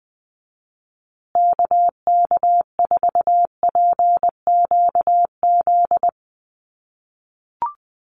Morse code, heard as a keyed tone, about 20 wpm, 700 Hz. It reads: KK4PQZ
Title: USA Call Signs Repeated 2x 20wpm